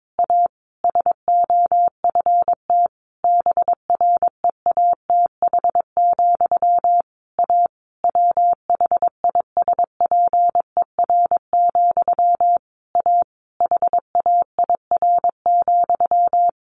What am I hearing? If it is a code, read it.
Morse code, heard as a keyed tone, about 22 wpm, 700 Hz. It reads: A SOFT BREATH, A WHISPER, A HAIR,
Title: A Sound Of Thunder 22wpm